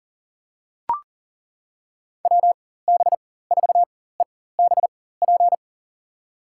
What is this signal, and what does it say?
Morse code, heard as a keyed tone, about 40 wpm, spaced as for 19 wpm, 700 Hz. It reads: WB4EBP
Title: USA Call Signs - 4x Character Spacing - 40wpm